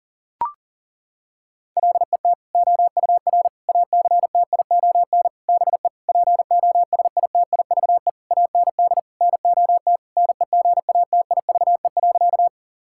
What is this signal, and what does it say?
Morse code, heard as a keyed tone, about 37 wpm, 700 Hz. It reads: LET OUR ACTION BE POSITIVE AND NOT NEGATIVE.
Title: Sentences from Top 1000 Words - Review Part 6 - 40wpm